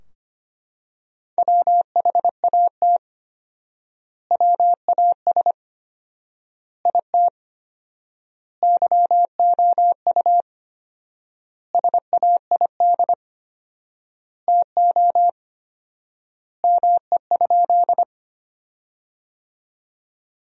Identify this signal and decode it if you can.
Morse code, heard as a keyed tone, about 25 wpm, 700 Hz. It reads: WHAT WAS IT YOU SAID TO ME?
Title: 4x Word-Spacing - Sentences from Top 200 Words - 25wpm